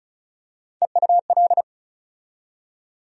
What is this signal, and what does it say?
Morse code, heard as a keyed tone, about 36 wpm, 700 Hz. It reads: EUL
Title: USA Call Sign Postfixes - 3 Letters 35wpm